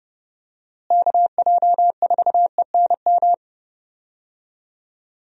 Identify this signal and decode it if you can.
Morse code, heard as a keyed tone, about 30 wpm, 700 Hz. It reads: KJ4ENM
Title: USA Call Signs - 6 Characters - Repeated 2x - 30wpm